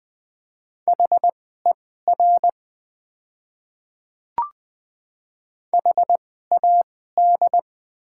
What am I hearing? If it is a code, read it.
Morse code, heard as a keyed tone, about 20 wpm, spaced as for 14 wpm, 700 Hz. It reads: HER HAD
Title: 2x Character Spacing - Top 100 Words - 20wpm